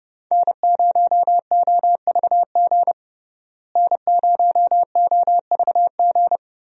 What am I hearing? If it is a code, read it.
Morse code, heard as a keyed tone, about 30 wpm, 700 Hz. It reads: N0OVG N0OVG
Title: USA Call Signs - 5 Characters - Repeated 3x - 30wpm